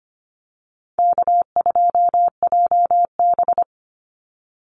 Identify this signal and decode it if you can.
Morse code, heard as a keyed tone, about 25 wpm, 700 Hz. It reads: K2JB
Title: NA SOTA Activator Callsigns 25wpm